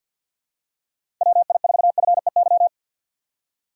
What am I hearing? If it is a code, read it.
Morse code, heard as a keyed tone, about 53 wpm, 700 Hz. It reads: WI4FEY
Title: USA Call Signs - 6 Characters - 6x Character Spacing - 50wpm